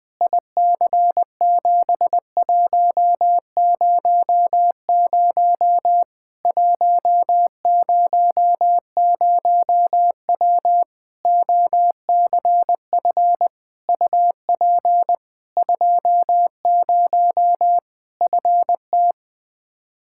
Morse code, heard as a keyed tone, about 20 wpm, 700 Hz. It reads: IC7100 100W OCF UP 20 FT
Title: Practice QSO - Exchange 2 - Info - 20wpm